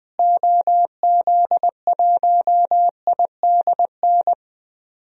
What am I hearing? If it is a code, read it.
Morse code, heard as a keyed tone, about 20 wpm, 700 Hz. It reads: OZ1IDN
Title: DXCC-100 Call Signs - 6 Characters - Repeated 3x - 20wpm